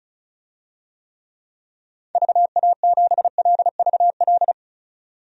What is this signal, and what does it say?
Morse code, heard as a keyed tone, about 36 wpm, 700 Hz. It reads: VA7LVL
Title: DXCC-100 Call Signs - 6 Characters - 35wpm